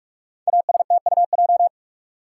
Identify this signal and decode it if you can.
Morse code, heard as a keyed tone, about 44 wpm, 700 Hz. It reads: ASTUJ
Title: Code Groups - 5 Letters - 45wpm